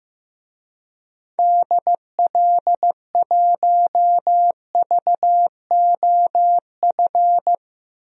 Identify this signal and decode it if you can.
Morse code, heard as a keyed tone, about 16 wpm, 700 Hz. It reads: DL1VOF
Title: DXCC-100 Call Signs - 6 Characters - 15wpm